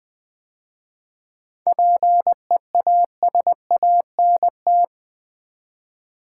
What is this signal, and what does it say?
Morse code, heard as a keyed tone, about 20 wpm, 700 Hz. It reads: PEASANT
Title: 7 Letter Words - First 3 Letters Given - 20wpm